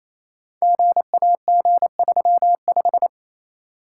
Morse code, heard as a keyed tone, about 27 wpm, 700 Hz. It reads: GAG35
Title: Code Groups - 5 Letter:Numbers - 28wpm